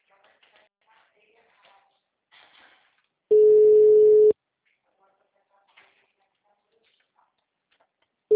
Ringback tone (Europe, 425 Hz).